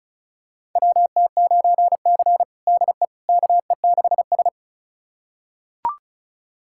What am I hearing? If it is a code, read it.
Morse code, heard as a keyed tone, about 34 wpm, 700 Hz. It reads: WT9C DE KE6S
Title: USA Callsign DE Callsign - 4 Characters - 35wpm